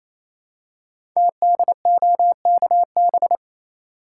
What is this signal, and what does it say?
Morse code, heard as a keyed tone, about 29 wpm, 700 Hz. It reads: TDOKB
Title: Code Groups - 5 Letters - 28wpm